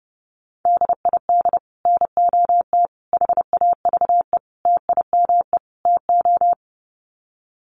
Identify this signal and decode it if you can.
Morse code, heard as a keyed tone, about 30 wpm, 700 Hz. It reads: DID NOT HAVE TIME TO
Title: Sets of 5 Words Encore 30wpm